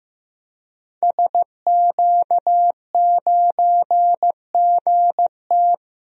Morse code, heard as a keyed tone, about 15 wpm, 700 Hz. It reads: SQ9GT
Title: DXCC-100 Call Signs - 5 Characters - Repeated 3x - 15wpm